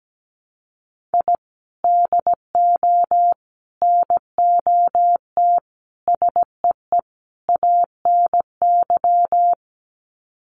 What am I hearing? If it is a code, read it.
Morse code, heard as a keyed tone, about 17 wpm, 700 Hz. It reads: I DO NOT SEE ANY
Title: Sets of 5 Words 17wpm